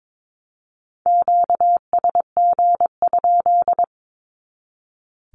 Morse code, heard as a keyed tone, about 22 wpm, 700 Hz. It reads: QSG?
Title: Q-Codes 22wpm